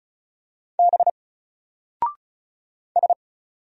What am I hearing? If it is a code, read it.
Morse code, heard as a keyed tone, about 36 wpm, 700 Hz. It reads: B S
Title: Single Character (Letter:Number:Punctuation:Prosign) Alt 35wpm